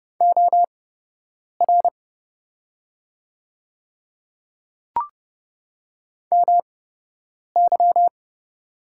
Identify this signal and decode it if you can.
Morse code, heard as a keyed tone, about 30 wpm, spaced as for 8 wpm, 700 Hz. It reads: OR MY
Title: 8x Character Spacing - Top 100 Words - 30wpm